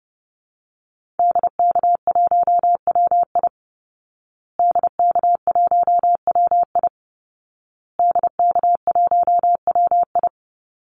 Morse code, heard as a keyed tone, about 30 wpm, 700 Hz. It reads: DK1WI DK1WI DK1WI
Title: Call Signs Repeated 2x - 30wpm